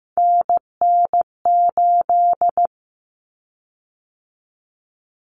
Morse code, heard as a keyed tone, about 15 wpm, 700 Hz. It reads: NN8
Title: Common Cut Numbers - 3-Digit 15wpm